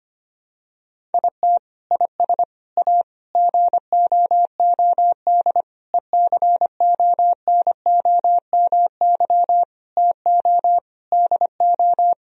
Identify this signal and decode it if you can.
Morse code, heard as a keyed tone, about 25 wpm, 700 Hz. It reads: IT IS A GOOD ECONOMY TO DO
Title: Sentences from Top 1000 Words - Review Part 3 - 25wpm